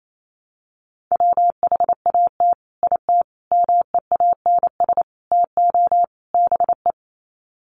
Morse code, heard as a keyed tone, about 28 wpm, 700 Hz. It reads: WHAT IT MEANS TO BE
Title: Sets of 5 Words 28wpm